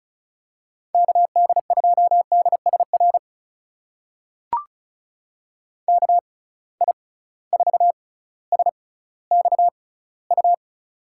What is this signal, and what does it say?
Morse code, heard as a keyed tone, about 36 wpm, spaced as for 18 wpm, 700 Hz. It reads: KD2DSR KI4SXU
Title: USA Call Signs - 6x Character Spacing - 35wpm